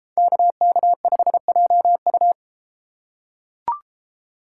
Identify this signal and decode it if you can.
Morse code, heard as a keyed tone, about 33 wpm, 700 Hz. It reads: KK5JU